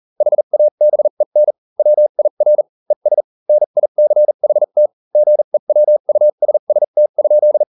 Morse code, heard as a keyed tone, about 40 wpm, spaced as for 35 wpm, 600 Hz. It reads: HABEN WIR ES NICHT GEWUSST?